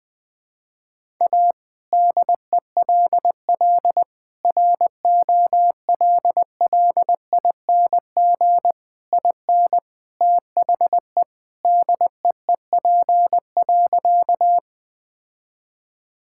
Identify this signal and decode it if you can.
Morse code, heard as a keyed tone, about 21 wpm, 700 Hz. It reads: A DELL ROLLING IN THE DEEP.
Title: All-Age Appropriate Jokes - 20wpm